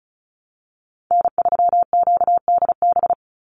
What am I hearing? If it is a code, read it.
Morse code, heard as a keyed tone, about 34 wpm, 700 Hz. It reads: N3QDB
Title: USA Call Signs 35wpm